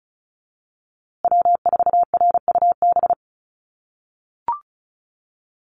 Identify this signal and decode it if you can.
Morse code, heard as a keyed tone, about 34 wpm, 700 Hz. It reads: W4RUB
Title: USA Call Signs Repeated 2x 35wpm